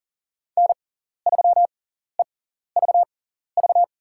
Morse code, heard as a keyed tone, about 43 wpm, spaced as for 14 wpm, 690 Hz. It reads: N3EVV
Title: USA Call Signs - 5 Characters - 6x Character Spacing - 40wpm